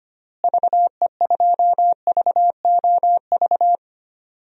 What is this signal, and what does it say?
Morse code, heard as a keyed tone, about 25 wpm, 700 Hz. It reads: VE2VOV
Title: DXCC-100 Call Signs - 6 Characters - Repeated 3x - 25wpm